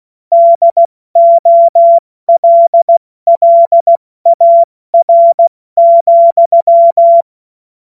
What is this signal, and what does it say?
Morse code, heard as a keyed tone, about 16 wpm, 680 Hz. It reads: DOLLAR,
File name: Morsekurs / Witze / Witz 21. Februar 2023 70 F80